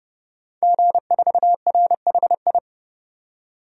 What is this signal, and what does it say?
Morse code, heard as a keyed tone, about 30 wpm, 700 Hz. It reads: G4RHI
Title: DXCC-100 Call Signs - 5 Characters - 30wpm